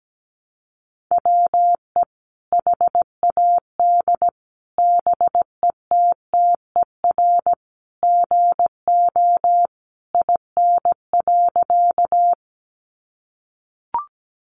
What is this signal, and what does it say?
Morse code, heard as a keyed tone, about 17 wpm, 700 Hz. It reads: WE HAD BETTER GO IN.
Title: Sentences from Top 400 Words 17wpm